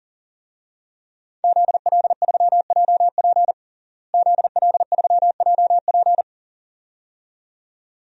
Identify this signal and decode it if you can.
Morse code, heard as a keyed tone, about 40 wpm, 700 Hz. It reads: ZL3JP ZL3JP
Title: DXCC-100 Call Signs - 5 Characters - Repeated 2x - 40wpm